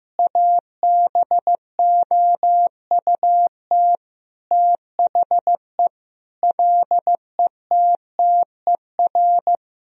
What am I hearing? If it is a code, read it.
Morse code, heard as a keyed tone, about 15 wpm, 700 Hz. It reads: ABOUT THE LETTER
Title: Sentences from Top 1000 Words - Review Part 2 - 15wpm